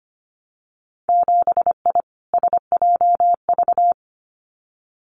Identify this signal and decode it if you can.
Morse code, heard as a keyed tone, about 25 wpm, 700 Hz. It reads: 7I SJV
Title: Winter Field Day Exchanges Encore 25wpm